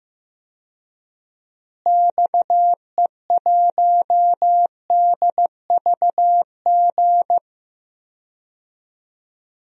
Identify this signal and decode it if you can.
Morse code, heard as a keyed tone, about 15 wpm, 700 Hz. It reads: XE1DVG
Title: DXCC-100 Call Signs - 6 Characters - Repeated 3x - 15wpm